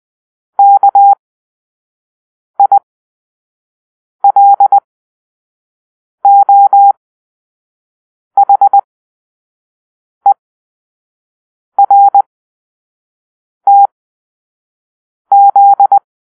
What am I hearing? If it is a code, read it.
Morse code, heard as a keyed tone, about 20 wpm, spaced as for 5 wpm, 800 Hz. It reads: KILOHERTZ